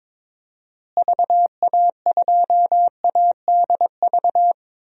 Morse code, heard as a keyed tone, about 22 wpm, 700 Hz. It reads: VA2ADV
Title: DXCC-100 Call Signs - 6 Characters - Repeated 3x - 22wpm